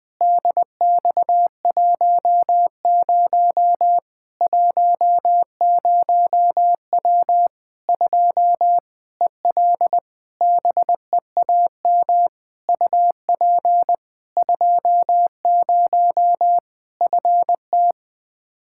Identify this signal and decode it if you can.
Morse code, heard as a keyed tone, about 21 wpm, 700 Hz. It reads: DX10 10W 2 EL BEAM UP 20 FT
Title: Practice QSO - Exchange 2 - Info - 20wpm